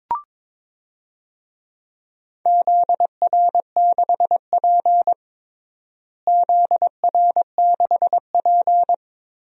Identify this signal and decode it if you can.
Morse code, heard as a keyed tone, about 22 wpm, 700 Hz. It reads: ZR6P ZR6P
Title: DXCC-100 Call Signs - 4 Characters - Repeated 3x - 22wpm